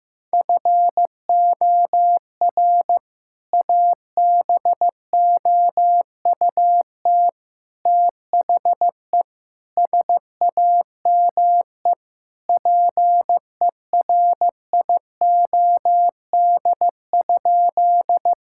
Morse code, heard as a keyed tone, about 15 wpm, 700 Hz. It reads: FOR ABOUT THE SAME PERIOD?
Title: Sentences from Top 1000 Words - Review Part 5 - 15wpm